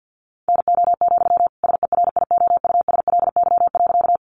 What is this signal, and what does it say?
Morse code, heard as a keyed tone, about 50 wpm, 700 Hz. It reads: NO, SERIOUSLY.
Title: Sentences from Top 1200 Words 50wpm